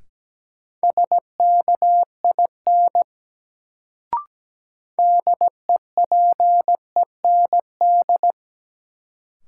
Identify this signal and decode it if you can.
Morse code, heard as a keyed tone, about 17 wpm, 700 Hz. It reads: SKIN DEPEND
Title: Sentences from Top 1300 Words - New Word Drill Alt - 17wpm